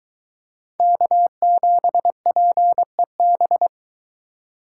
Morse code, heard as a keyed tone, about 23 wpm, 700 Hz. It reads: K7PEB